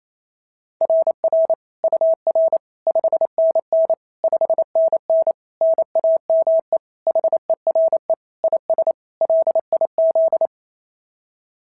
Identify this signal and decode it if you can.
Morse code, heard as a keyed tone, about 27 wpm, 650 Hz. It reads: RR UR 5NN 5NN NAME HERE IS LIZ
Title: SOTA QSO Practice 28wpm